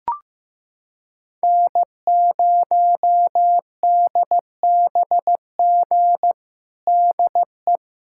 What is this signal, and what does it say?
Morse code, heard as a keyed tone, about 16 wpm, 700 Hz. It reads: N0DBG DE
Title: USA Callsign DE Callsign 15wpm